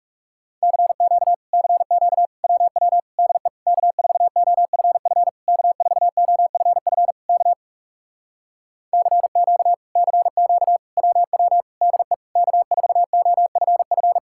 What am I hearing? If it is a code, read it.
Morse code, heard as a keyed tone, about 40 wpm, 700 Hz. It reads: CQ CQ WW DE K4OFF K4OFF K CQ CQ WW DE K4OFF